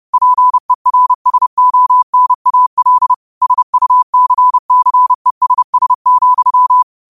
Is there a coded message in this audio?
Yes: Morse code from 0.1 to 6.8 s, heard as a keyed tone, about 30 wpm, 1000 Hz. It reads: PERSONAL SUCCESS,